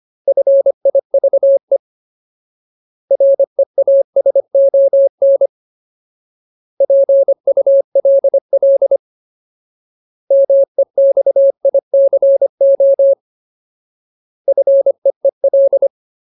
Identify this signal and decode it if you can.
Morse code, heard as a keyed tone, about 25 wpm, 550 Hz. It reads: FIVE REASON PULL MEXICO FEEL
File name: Ultimate Word List / 25WPM Ultimate